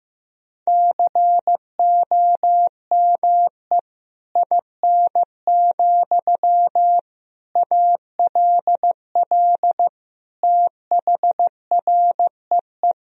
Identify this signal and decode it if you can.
Morse code, heard as a keyed tone, about 15 wpm, 700 Hz. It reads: COME IN, ALL THREE
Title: Sentences from Top 1000 Words - Review Part 1 - 15wpm